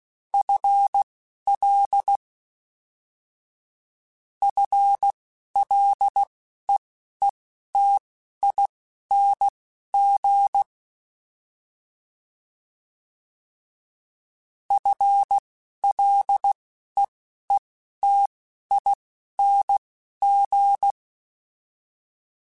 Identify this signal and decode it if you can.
Morse code, heard as a keyed tone, about 16 wpm, spaced as for 12 wpm, 800 Hz. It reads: FL FLEETING FLEETING
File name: FL Prefixes 3x2 Twice @10